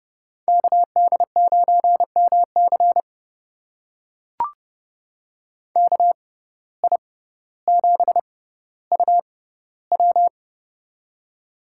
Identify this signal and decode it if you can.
Morse code, heard as a keyed tone, about 30 wpm, spaced as for 15 wpm, 700 Hz. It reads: KD9MC KI7UW
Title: USA Call Signs - 5 Characters - 6x Character Spacing - 30wpm